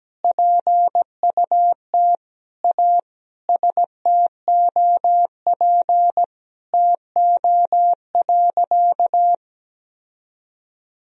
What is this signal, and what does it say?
Morse code, heard as a keyed tone, about 17 wpm, 700 Hz. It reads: PUT A STOP TO.
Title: Sentences from Top 1000 Words - Review Part 1 - 17wpm